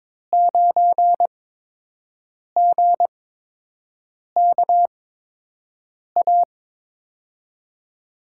Morse code, heard as a keyed tone, about 23 wpm, spaced as for 6 wpm, 700 Hz. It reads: 9GKA
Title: USA Call Signs - 5 Characters - 8x Character Spacing - 22wpm